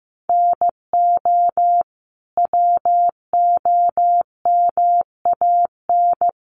Morse code, heard as a keyed tone, about 15 wpm, 700 Hz. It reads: NO WOMAN
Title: Sentences from Top 1100 Words 15wpm